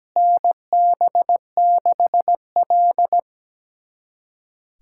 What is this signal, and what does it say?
Morse code, heard as a keyed tone, about 17 wpm, 700 Hz. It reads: NB6L
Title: USA Callsign DE Callsign - 4 Characters - 17wpm